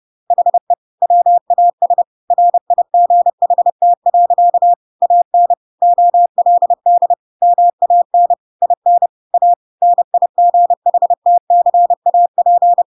Morse code, heard as a keyed tone, about 30 wpm, 700 Hz. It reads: HE WAS RIGHT. AN OLD MAN IN A NIGHTCAP
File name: Pinocchio / Pinocchio ch6 30 30 ik1hge